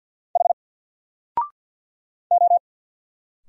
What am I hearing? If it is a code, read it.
Morse code, heard as a keyed tone, about 49 wpm, 700 Hz. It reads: H O
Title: Single Character (Letter:Number:Punctuation:Prosign) Alt 50wpm